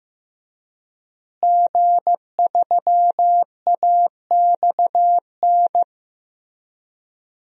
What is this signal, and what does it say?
Morse code, heard as a keyed tone, about 15 wpm, 700 Hz. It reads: G3AXN
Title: DXCC-100 Call Signs - 5 Characters - Repeated 3x - 15wpm